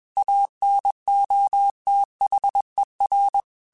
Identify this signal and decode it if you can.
Morse code, heard as a keyed tone, about 21 wpm, 800 Hz. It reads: ANOTHER
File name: QSOs / QSOs-mp3s / WA7QOX 1@20